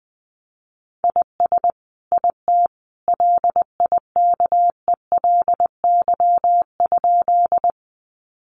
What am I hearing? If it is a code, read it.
Morse code, heard as a keyed tone, about 20 wpm, 700 Hz. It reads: IS IT LIKELY?
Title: Sentences from Top 600 Words 20wpm